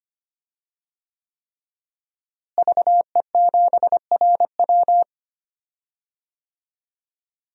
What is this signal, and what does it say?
Morse code, heard as a keyed tone, about 25 wpm, 700 Hz. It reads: VE7RW
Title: DXCC-100 Call Signs - 5 Characters - Repeated 3x - 25wpm